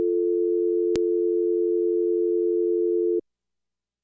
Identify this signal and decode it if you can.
Dial tone (North America), 350 + 440 Hz.